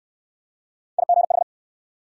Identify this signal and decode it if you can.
Morse code, heard as a keyed tone, about 61 wpm, 700 Hz. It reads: IGH